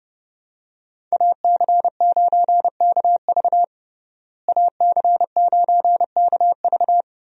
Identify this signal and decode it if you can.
Morse code, heard as a keyed tone, about 30 wpm, 700 Hz. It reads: AC9KV AC9KV
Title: USA Call Signs - 5 Characters - Repeated 2x - 30wpm